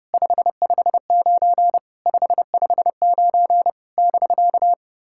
Morse code, heard as a keyed tone, about 30 wpm, 690 Hz. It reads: 559 559 <BK>
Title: SOTA QSO Practice 30wpm